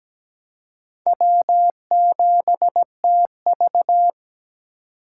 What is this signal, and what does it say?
Morse code, heard as a keyed tone, about 20 wpm, 700 Hz. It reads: W7TV
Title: Speed Racing - USA 1x2 Call Signs 17wpm